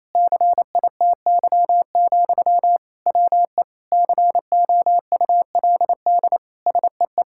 Morse code, heard as a keyed tone, about 28 wpm, 700 Hz. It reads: CITY, WE COULD SEE
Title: Sentences from Top 1000 Words - Review Part 5 - 28wpm